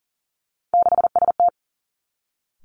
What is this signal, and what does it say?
Morse code, heard as a keyed tone, about 42 wpm, 700 Hz. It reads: 6ST
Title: Three Challenging Letter-Number Combinations 40wpm